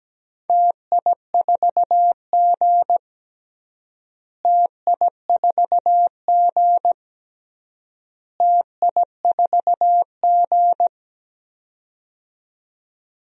Morse code, heard as a keyed tone, about 17 wpm, 700 Hz. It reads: TI4G TI4G TI4G
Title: DXCC-100 Call Signs - 4 Characters - Repeated 3x - 17wpm